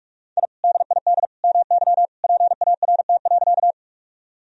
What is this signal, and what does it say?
Morse code, heard as a keyed tone, about 46 wpm, 700 Hz. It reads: I DID MY PART.
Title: Sentences from Top 1000 Words - Review Part 1 - 45wpm